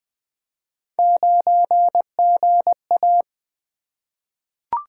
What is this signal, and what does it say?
Morse code, heard as a keyed tone, about 20 wpm, 700 Hz. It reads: 9GA
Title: Three Letter-Number Combinations - 20wpm